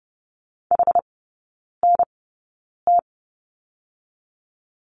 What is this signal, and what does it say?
Morse code, heard as a keyed tone, about 30 wpm, spaced as for 9 wpm, 700 Hz. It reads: HNT
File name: Lesson 18e - Letter m - Three Characters 30wpm